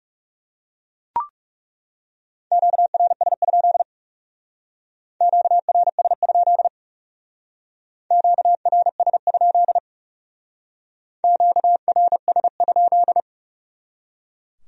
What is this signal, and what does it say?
Morse code, heard as a keyed tone, about 37 wpm, 700 Hz. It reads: QRS? QRS? QRS? QRS?
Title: Speed-Racing Q-Codes - 30wpm